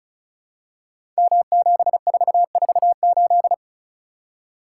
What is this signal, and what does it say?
Morse code, heard as a keyed tone, about 36 wpm, 700 Hz. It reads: M7448
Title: Code Groups - 5 Letter:Numbers - 35wpm